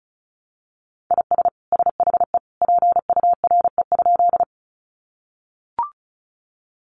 Morse code, heard as a keyed tone, about 34 wpm, 700 Hz. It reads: IS SHE PURE?
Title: Sentences from Top 1500 Words 35wpm